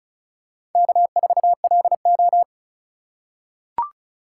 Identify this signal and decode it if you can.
Morse code, heard as a keyed tone, about 36 wpm, 700 Hz. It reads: K4LO
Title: USA Call Signs - 2x Character Spacing - 35wpm